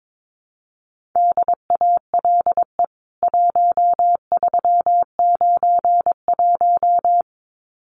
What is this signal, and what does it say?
Morse code, heard as a keyed tone, about 22 wpm, 700 Hz. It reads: DALE 1391
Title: CWT Exchange 22wpm